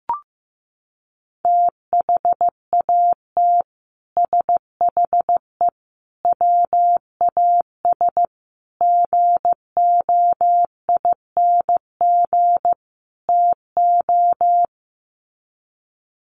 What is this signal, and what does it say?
Morse code, heard as a keyed tone, about 15 wpm, 700 Hz. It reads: THAT SHE WAS GOING TO
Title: Sets of 5 Words Encore2 15wpm